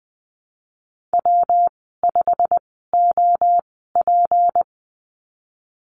Morse code, heard as a keyed tone, about 20 wpm, spaced as for 15 wpm, 700 Hz. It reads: W5OP
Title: 167 Lesson 27g - Number 7 - Call Signs 20wpm